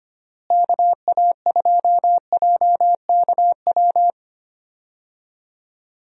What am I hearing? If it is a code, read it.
Morse code, heard as a keyed tone, about 25 wpm, 700 Hz. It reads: KA2JKW
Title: USA Call Signs - 6 Characters - Repeated 3x - 25wpm